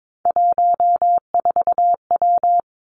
Morse code, heard as a keyed tone, about 22 wpm, 700 Hz. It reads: 14W
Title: Three Challenging Letter-Number Combinations 22wpm